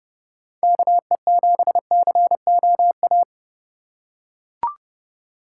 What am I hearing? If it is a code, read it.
Morse code, heard as a keyed tone, about 32 wpm, 700 Hz. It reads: KE7COA
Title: USA Call Signs - 4x Character Spacing - 30wpm